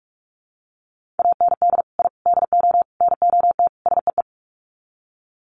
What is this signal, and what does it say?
Morse code, heard as a keyed tone, about 44 wpm, 700 Hz. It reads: AND I DO NOT SEE
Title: Sets of 5 Words Encore2 45wpm